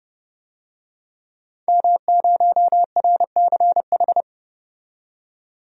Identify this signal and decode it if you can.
Morse code, heard as a keyed tone, about 30 wpm, 700 Hz. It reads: M0RCH
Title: DXCC-100 Call Signs - 5 Characters - 30wpm